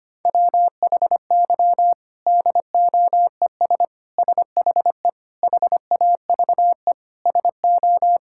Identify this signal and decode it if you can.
Morse code, heard as a keyed tone, about 25 wpm, 700 Hz. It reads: WHY DOES SHE HAVE SO
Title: Sentences from Top 1000 Words - Review Part 6 - 25wpm